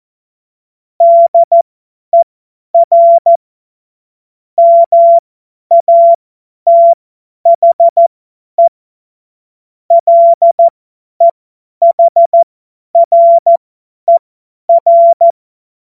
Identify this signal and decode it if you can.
Morse code, heard as a keyed tone, about 14 wpm, spaced as for 10 wpm, 680 Hz. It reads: DER MATHE LEHRER
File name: Morsekurs / Witze / Witz 17. August 2021, 50 F70